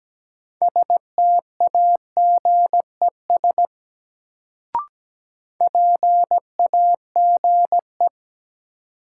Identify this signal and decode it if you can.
Morse code, heard as a keyed tone, about 17 wpm, 700 Hz. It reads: STAGES PAGE
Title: Sets of 5 Words Encore - New Word Drill - 17wpm